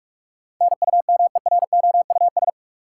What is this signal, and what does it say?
Morse code, heard as a keyed tone, about 42 wpm, 700 Hz. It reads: NUMEROUS
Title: Sentences from Top 1400 Words - New Word Drill Alt - 45wpm